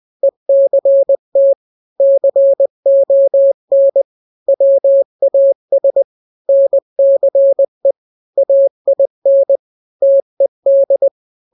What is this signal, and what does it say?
Morse code, heard as a keyed tone, about 20 wpm, 550 Hz. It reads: ECT CON WAS NCE AIN TED
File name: n-grams / 3-grams 20 18 WPM